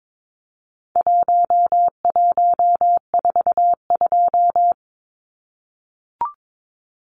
Morse code, heard as a keyed tone, about 22 wpm, 700 Hz. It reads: 1142
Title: Numbers - Contest Style - 4-Digit 22wpm